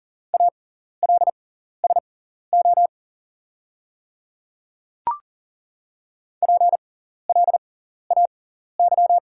Morse code, heard as a keyed tone, about 40 wpm, spaced as for 14 wpm, 690 Hz. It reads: ALSO PLAY
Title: 6x Character Spacing - Top 300 Words - 40wpm